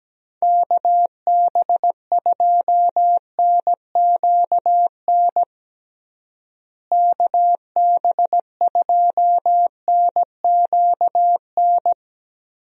Morse code, heard as a keyed tone, about 17 wpm, 700 Hz. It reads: KB2NQN KB2NQN
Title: USA Call Signs - 6 Characters - Repeated 2x - 17wpm